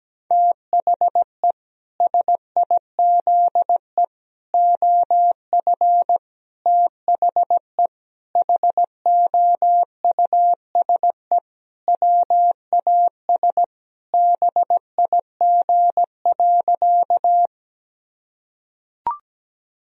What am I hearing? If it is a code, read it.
Morse code, heard as a keyed tone, about 17 wpm, 700 Hz. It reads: THE SIZE OF THE HOUSE WAS BIG.
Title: Sentences from Top 1000 Words - Review Part 2 - 17wpm